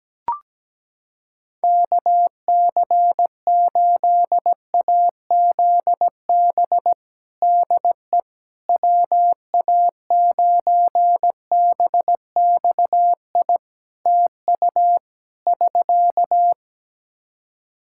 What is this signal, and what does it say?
Morse code, heard as a keyed tone, about 17 wpm, 700 Hz. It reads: KC8AZB DE WA9BXI TU <SK>
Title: K1USN SST QSO Practice 17wpm